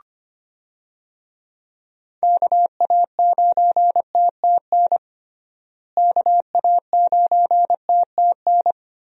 Morse code, heard as a keyed tone, about 25 wpm, 700 Hz. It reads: KA9TTN KA9TTN
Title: USA Call Signs - 6 Characters - Repeated 2x - 25wpm